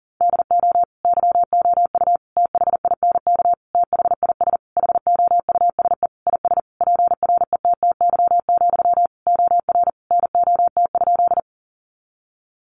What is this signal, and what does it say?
Morse code, heard as a keyed tone, about 40 wpm, 700 Hz. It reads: DO YOU THINK THIS HOUSE IS PRETTY, OR NOT?
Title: Sentences from Top 1000 Words 40wpm